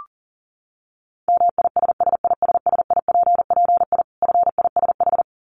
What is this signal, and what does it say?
Morse code, heard as a keyed tone, about 40 wpm, 700 Hz. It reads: MISSISSIPPI FISH
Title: Dit Challenge 40wpm